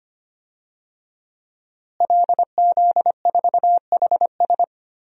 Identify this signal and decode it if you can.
Morse code, heard as a keyed tone, about 25 wpm, 700 Hz. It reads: LZ4HS
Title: DXCC-100 Call Signs - 5 Characters - Repeated 3x - 25wpm